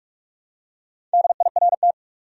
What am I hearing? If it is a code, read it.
Morse code, heard as a keyed tone, about 46 wpm, 700 Hz. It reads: DIRT
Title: Four Letter Words - 45wpm